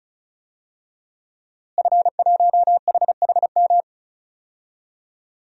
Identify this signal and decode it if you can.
Morse code, heard as a keyed tone, about 34 wpm, 700 Hz. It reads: F1HHM
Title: DXCC-100 Call Signs - 5 Characters - Repeated 2x - 35wpm